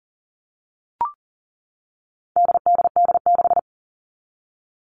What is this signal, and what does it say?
Morse code, heard as a keyed tone, about 40 wpm, 700 Hz. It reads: DDD6
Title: All Cut Numbers - 4-Digit 40wpm